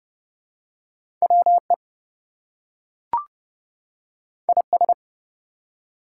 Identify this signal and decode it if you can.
Morse code, heard as a keyed tone, about 29 wpm, 700 Hz. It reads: WE IS
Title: Two Letter Words - 30wpm